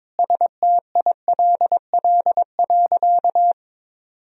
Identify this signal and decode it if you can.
Morse code, heard as a keyed tone, about 22 wpm, 700 Hz. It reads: STILL.
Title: Sentences from Top 1000 Words - Review Part 1 - 22wpm